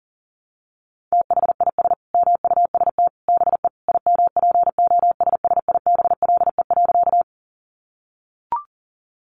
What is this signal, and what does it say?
Morse code, heard as a keyed tone, about 37 wpm, 700 Hz. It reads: THIS MUST BE IMPOSSIBLE.
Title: Sentences from Top 1100 Words 40wpm